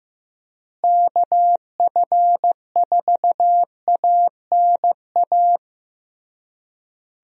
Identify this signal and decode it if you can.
Morse code, heard as a keyed tone, about 15 wpm, 700 Hz. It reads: KF4ANA
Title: USA Call Signs - 6 Characters - Repeated 3x - 15wpm